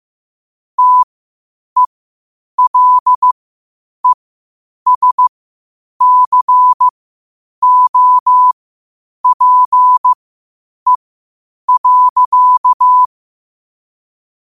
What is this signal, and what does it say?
Morse code, heard as a keyed tone, about 15 wpm, spaced as for 8 wpm, 1000 Hz. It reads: TELESCOPE.